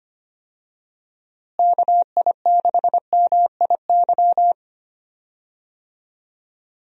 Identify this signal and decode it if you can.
Morse code, heard as a keyed tone, about 25 wpm, 700 Hz. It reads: KI6MIY
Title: USA Call Signs - 6 Characters - Repeated 2x - 25wpm